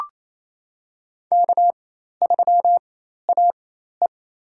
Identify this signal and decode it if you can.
Morse code, heard as a keyed tone, about 27 wpm, spaced as for 13 wpm, 700 Hz. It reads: K3AE
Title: USA Call Signs - 4 Characters - 4x Character Spacing - 28wpm